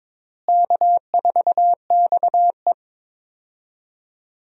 Morse code, heard as a keyed tone, about 22 wpm, 700 Hz. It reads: K4XE
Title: USA Call Signs - 4 Characters - Repeated 2x - 22wpm